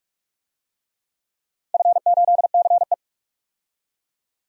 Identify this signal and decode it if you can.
Morse code, heard as a keyed tone, about 44 wpm, 700 Hz. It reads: F8CE
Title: DXCC-100 Call Signs - 4 Characters - 45wpm